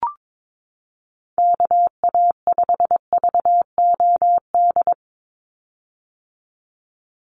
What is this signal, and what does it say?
Morse code, heard as a keyed tone, about 22 wpm, 700 Hz. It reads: KA5VOD
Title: USA Call Signs 22wpm